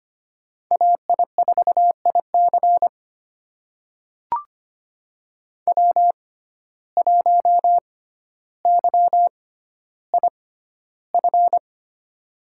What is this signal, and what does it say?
Morse code, heard as a keyed tone, about 25 wpm, spaced as for 13 wpm, 700 Hz. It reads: AI4IC W1YIF
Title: USA Call Signs - 5 Characters - 6x Character Spacing - 25wpm